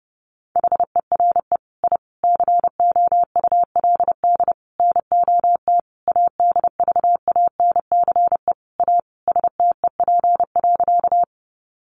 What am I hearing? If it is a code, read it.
Morse code, heard as a keyed tone, about 30 wpm, 700 Hz. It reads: HERE I COULD NOT ADVANCE A STEP.
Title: Sentences from Top 1400 Words 30wpm